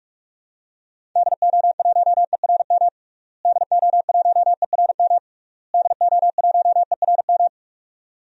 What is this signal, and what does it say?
Morse code, heard as a keyed tone, about 46 wpm, 700 Hz. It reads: DO1ERM DO1ERM DO1ERM
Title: DXCC-100 Call Signs - 6 Characters - Repeated 3x - 45wpm